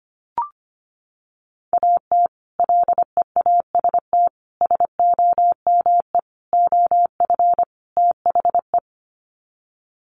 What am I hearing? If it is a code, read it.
Morse code, heard as a keyed tone, about 25 wpm, 700 Hz. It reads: AT LEAST SOME OF THE
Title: Sets of 5 Words Encore 25wpm